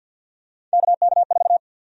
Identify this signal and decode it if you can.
Morse code, heard as a keyed tone, about 52 wpm, 700 Hz. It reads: KK4
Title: USA Call Sign Prefixes - 50wpm